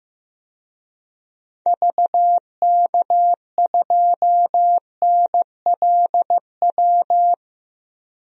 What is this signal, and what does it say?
Morse code, heard as a keyed tone, about 15 wpm, 700 Hz. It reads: VK2NLW
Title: DXCC-100 Call Signs - 6 Characters - 15wpm